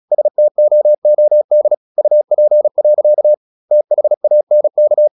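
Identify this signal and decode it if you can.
Morse code, heard as a keyed tone, about 36 wpm, 600 Hz. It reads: STOOD UP. THANK
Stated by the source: Morse practice at 36 wpm, text of Ken Follett's novel Never, from CW med Gnister.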